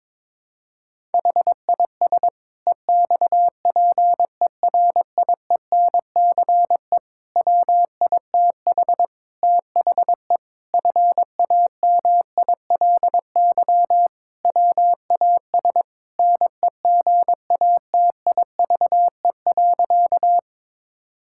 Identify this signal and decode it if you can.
Morse code, heard as a keyed tone, about 22 wpm, 700 Hz. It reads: HIS EXPERIENCE WITH THE FAMILY WAS NEGATIVE.
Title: Sentences from Top 1000 Words - Review Part 6 - 22wpm